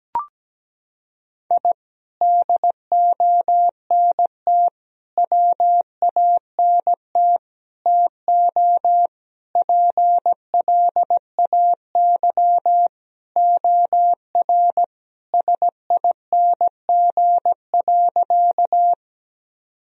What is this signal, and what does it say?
Morse code, heard as a keyed tone, about 17 wpm, 700 Hz. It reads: I DONT WANT TO PLAY OR SING.
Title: Sentences from Top 1000 Words - Review Part 2 - 17wpm